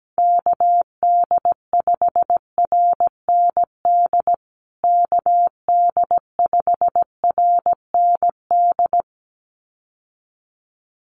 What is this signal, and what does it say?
Morse code, heard as a keyed tone, about 17 wpm, 700 Hz. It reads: KD5RND KD5RND
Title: USA Call Signs Repeated 2x 17wpm